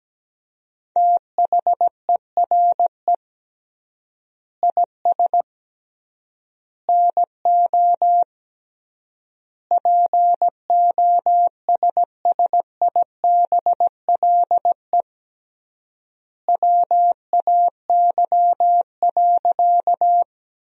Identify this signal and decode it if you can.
Morse code, heard as a keyed tone, about 17 wpm, 700 Hz. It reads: THERE IS NO POSSIBLE WAY.
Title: 3x Word-Spacing - Sentences from Top 500 Words - 17wpm